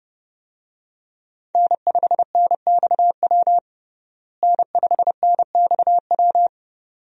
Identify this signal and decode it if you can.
Morse code, heard as a keyed tone, about 30 wpm, 700 Hz. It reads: N5NXW N5NXW
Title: USA Call Signs - 5 Characters - Repeated 2x - 30wpm